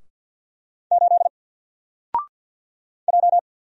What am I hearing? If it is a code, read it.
Morse code, heard as a keyed tone, about 48 wpm, 700 Hz. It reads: 8 J
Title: Single Character (Letter:Number:Punctuation:Prosign) 50wpm